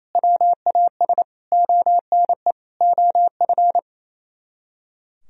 Morse code, heard as a keyed tone, about 30 wpm, 700 Hz. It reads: WAS ONE OF
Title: Speed-Racing - Sets of 4 Words 28wpm